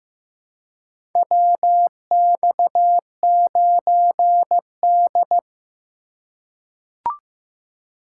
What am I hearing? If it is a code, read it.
Morse code, heard as a keyed tone, about 15 wpm, 700 Hz. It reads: WX9D
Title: USA Call Signs - 4 Characters - 2x Character Spacing - 15wpm